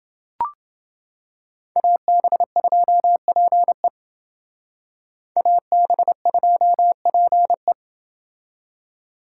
Morse code, heard as a keyed tone, about 29 wpm, 700 Hz. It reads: AB2PE AB2PE